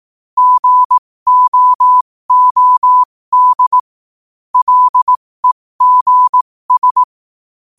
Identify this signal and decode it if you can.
Morse code, heard as a keyed tone, about 18 wpm, spaced as for 15 wpm, 1000 Hz. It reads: GOOD LEGS
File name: qotdfiles / 15wpm